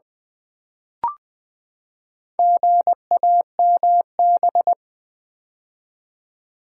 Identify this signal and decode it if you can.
Morse code, heard as a keyed tone, about 20 wpm, 700 Hz. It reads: GAMB
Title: Build A Word - 4 Letters At A Time - 8-Letter Words - 20wpm